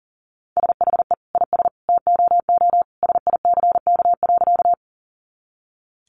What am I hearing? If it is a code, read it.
Morse code, heard as a keyed tone, about 40 wpm, 700 Hz. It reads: SHE IS TOO SICK.
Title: Easy Phrases 40wpm